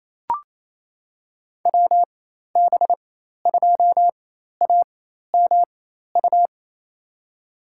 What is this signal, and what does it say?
Morse code, heard as a keyed tone, about 28 wpm, spaced as for 13 wpm, 700 Hz. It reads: WB2AMU
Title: USA Call Signs - 6 Characters - 4x Character Spacing - 28wpm